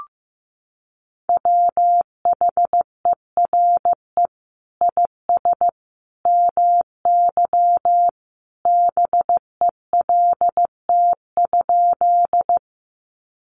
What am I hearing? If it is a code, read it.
Morse code, heard as a keyed tone, about 15 wpm, 700 Hz. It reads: WHERE IS MY BELT?
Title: Sentences from Top 1400 Words 15wpm